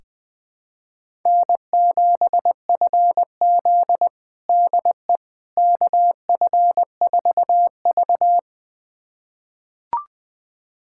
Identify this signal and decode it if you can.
Morse code, heard as a keyed tone, about 20 wpm, 700 Hz. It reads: N7FZ DE KF4V
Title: USA Callsign DE Callsign - 4 Characters - 20wpm